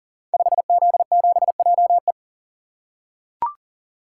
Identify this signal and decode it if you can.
Morse code, heard as a keyed tone, about 40 wpm, 700 Hz. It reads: 5Z7JE
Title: Code Groups - 5 Letter:Numbers - 40wpm